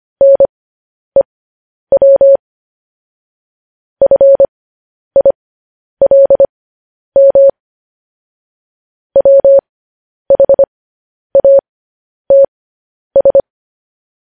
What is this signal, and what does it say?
Morse code, heard as a keyed tone, about 25 wpm, spaced as for 10 wpm, 550 Hz. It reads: NEW FILM WHATS
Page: news-headlines-in-morse-code-at-10-wpm